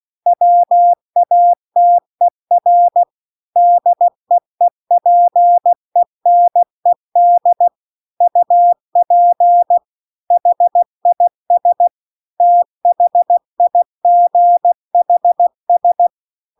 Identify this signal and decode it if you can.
Morse code, heard as a keyed tone, about 16 wpm, 700 Hz. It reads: WATER DEEPENED UP HIS THIGHS
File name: CW-The-Big-Twoheaded-River-2-16WPM